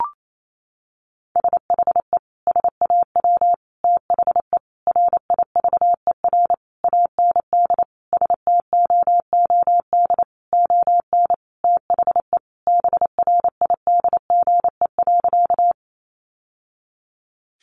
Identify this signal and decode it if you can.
Morse code, heard as a keyed tone, about 28 wpm, 700 Hz. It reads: SHE SAW THE RIVER AND STOOD ON THE BRIDGE.
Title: Sentences from Top 1200 Words 28wpm